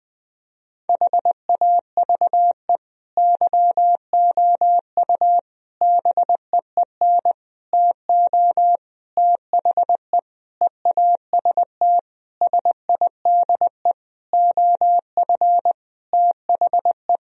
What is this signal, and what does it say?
Morse code, heard as a keyed tone, about 21 wpm, 700 Hz. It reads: HAVE YOU BEEN TO THE EAST SIDE OF THE
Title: Sentences from Top 1000 Words - Review Part 3 - 20wpm